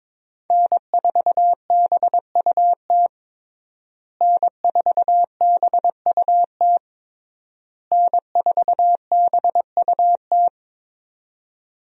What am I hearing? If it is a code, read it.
Morse code, heard as a keyed tone, about 22 wpm, 700 Hz. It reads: N4BUT N4BUT N4BUT
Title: USA Call Signs - 5 Characters - Repeated 3x - 22wpm